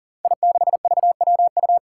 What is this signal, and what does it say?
Morse code, heard as a keyed tone, about 39 wpm, 700 Hz. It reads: I6VWU
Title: Five Challenging Letter-Number Combinations 40wpm